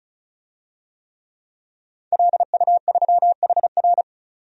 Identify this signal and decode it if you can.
Morse code, heard as a keyed tone, about 36 wpm, 700 Hz. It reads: LU3HR
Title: DXCC-100 Call Signs - 5 Characters - 35wpm